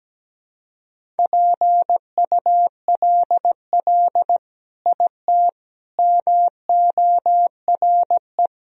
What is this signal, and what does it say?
Morse code, heard as a keyed tone, about 17 wpm, 700 Hz. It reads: PULL IT MORE
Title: Sentences from Top 400 Words and Abbreviations 17wpm